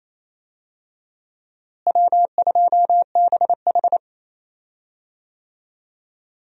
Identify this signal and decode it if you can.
Morse code, heard as a keyed tone, about 28 wpm, 700 Hz. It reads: W2BH
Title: USA Call Signs - 4 Characters - Repeated 3x - 28wpm